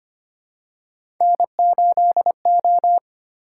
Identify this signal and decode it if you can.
Morse code, heard as a keyed tone, about 25 wpm, 700 Hz. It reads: N8O
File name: Three Letter-Number Combinations - 25wpm